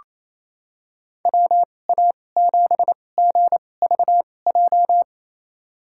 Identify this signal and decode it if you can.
Morse code, heard as a keyed tone, about 28 wpm, spaced as for 20 wpm, 700 Hz. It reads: WA7GVJ
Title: USA Call Signs - 6 Characters - 2x Character Spacing - 28wpm